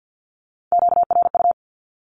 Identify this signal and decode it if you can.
Morse code, heard as a keyed tone, about 48 wpm, 700 Hz. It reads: QRU
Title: Q-Codes Repeated 3x 50wpm